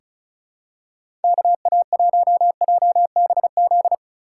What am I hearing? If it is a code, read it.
Morse code, heard as a keyed tone, about 36 wpm, 700 Hz. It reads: KA1JBZ
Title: USA Call Signs - 6 Characters - 3x Character Spacing - 35wpm